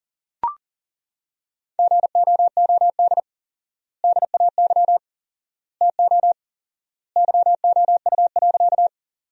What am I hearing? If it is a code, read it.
Morse code, heard as a keyed tone, about 40 wpm, 700 Hz. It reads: GOOD DAY TO YOU.
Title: 4x Word-Spacing - Sentences from Top 200 Words - 40wpm